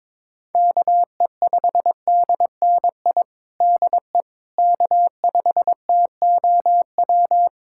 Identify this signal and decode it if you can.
Morse code, heard as a keyed tone, about 22 wpm, 700 Hz. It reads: KE5DNI DE K5TOW
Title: USA Callsign DE Callsign 22wpm